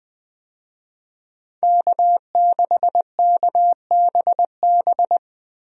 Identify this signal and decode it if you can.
Morse code, heard as a keyed tone, about 21 wpm, 700 Hz. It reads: K6KBB
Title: USA Call Signs - 5 Characters - Repeated 3x - 20wpm